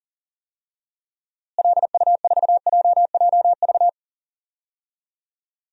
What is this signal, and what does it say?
Morse code, heard as a keyed tone, about 40 wpm, 700 Hz. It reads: LU4JJV
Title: DXCC-100 Call Signs - 6 Characters - Repeated 3x - 40wpm